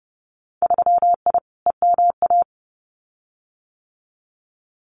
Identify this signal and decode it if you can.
Morse code, heard as a keyed tone, about 30 wpm, 700 Hz. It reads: 3I EMA
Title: Winter Field Day Exchanges Encore 30wpm